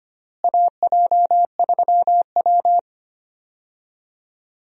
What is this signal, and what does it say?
Morse code, heard as a keyed tone, about 25 wpm, 700 Hz. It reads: AJ3W
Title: USA Call Signs - 4 Characters - Repeated 2x - 25wpm